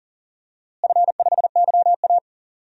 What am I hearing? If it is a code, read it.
Morse code, heard as a keyed tone, about 40 wpm, 700 Hz. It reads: F5YA
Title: DXCC-100 Call Signs - 4 Characters - Repeated 2x - 40wpm